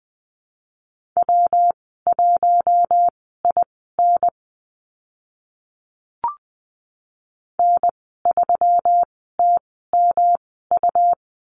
Morse code, heard as a keyed tone, about 20 wpm, spaced as for 15 wpm, 700 Hz. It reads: W1IN N3TMU
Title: 140 Lesson 23g - Letter f - Call Signs 20wpm